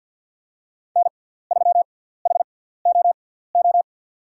Morse code, heard as a keyed tone, about 49 wpm, spaced as for 17 wpm, 700 Hz. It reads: N3HOO
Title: USA Call Signs - 5 Characters - 6x Character Spacing - 50wpm